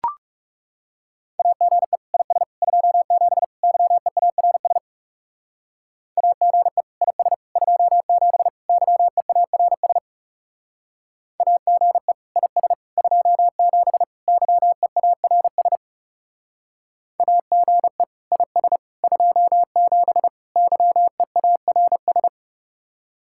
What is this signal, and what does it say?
Morse code, heard as a keyed tone, about 40 wpm, 700 Hz. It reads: AGE IS 27 YEARS AGE IS 27 YEARS AGE IS 27 YEARS AGE IS 27 YEARS